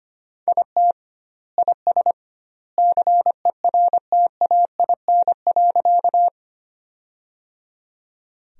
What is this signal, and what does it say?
Morse code, heard as a keyed tone, about 25 wpm, 700 Hz. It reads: IT IS CERTAIN.
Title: 2x Word-Spacing - Sentences from Top 500 Words - 25wpm